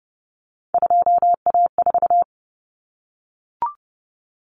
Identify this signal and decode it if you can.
Morse code, heard as a keyed tone, about 30 wpm, 700 Hz. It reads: 2A4
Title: Common Cut Numbers - 3-Digit 30wpm